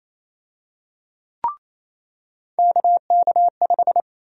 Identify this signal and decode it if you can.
Morse code, heard as a keyed tone, about 28 wpm, 700 Hz. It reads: KK5
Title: USA Call Sign Prefixes - 28wpm